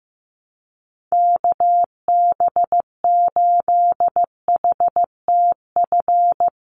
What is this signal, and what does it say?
Morse code, heard as a keyed tone, about 15 wpm, 700 Hz. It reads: KB8HTF
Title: USA Call Signs 15wpm